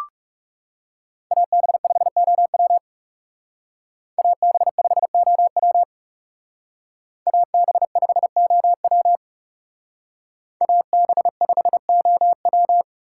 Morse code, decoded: AB5OW AB5OW AB5OW AB5OW

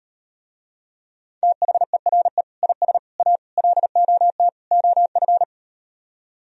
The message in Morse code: THERE IS A LOT OF

